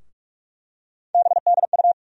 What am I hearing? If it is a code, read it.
Morse code, heard as a keyed tone, about 46 wpm, 700 Hz. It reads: BDU
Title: USA Call Sign Postfixes - 3 Letters 45wpm